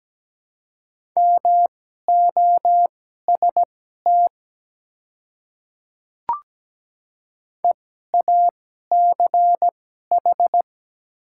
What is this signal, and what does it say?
Morse code, heard as a keyed tone, about 17 wpm, spaced as for 12 wpm, 690 Hz. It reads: MOST EACH
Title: 2x Character Spacing - Top 300 Words - 17wpm